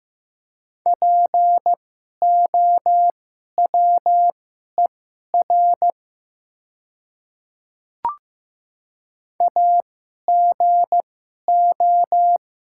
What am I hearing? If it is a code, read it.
Morse code, heard as a keyed tone, about 15 wpm, spaced as for 11 wpm, 700 Hz. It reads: POWER AGO
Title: 2x Character Spacing - Top 500 Words - 15wpm